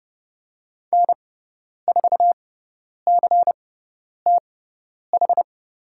Morse code, heard as a keyed tone, about 30 wpm, spaced as for 10 wpm, 700 Hz. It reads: N4CTH
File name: Lesson 16g - Letter c - Call Signs 30wpm